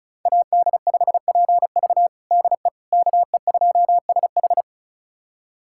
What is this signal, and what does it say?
Morse code, heard as a keyed tone, about 36 wpm, 700 Hz. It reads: AD5PV DE KE2SH
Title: USA Callsign DE Callsign - 5 Characters - 35wpm